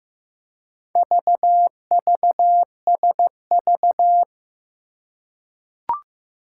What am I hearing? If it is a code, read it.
Morse code, heard as a keyed tone, about 15 wpm, 700 Hz. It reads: VVSV
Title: Four Challenging Letter-Number Combinations 15wpm